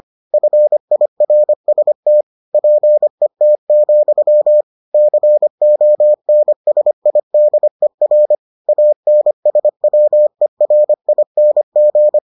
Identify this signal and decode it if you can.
Morse code, heard as a keyed tone, about 25 wpm, 600 Hz. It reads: FIRST PET, CONSIDER ANSWERING